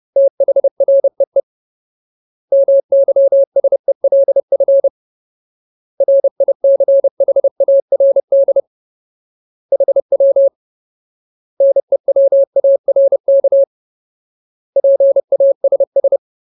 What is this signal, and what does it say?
Morse code, heard as a keyed tone, about 30 wpm, 550 Hz. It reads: THREE MYSELF RICHARD HW NEWARK PASS
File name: Ultimate Word List / 30WPM Ultimate